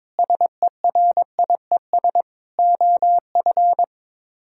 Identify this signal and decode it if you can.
Morse code, heard as a keyed tone, about 22 wpm, 700 Hz. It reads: SERIES OF
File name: practice / 22wpm / Sets of 7 Words - 22wpm